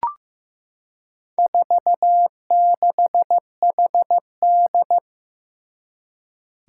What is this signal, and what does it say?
Morse code, heard as a keyed tone, about 15 wpm, 700 Hz. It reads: 46HD
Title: Four Challenging Letter-Number Combinations 15wpm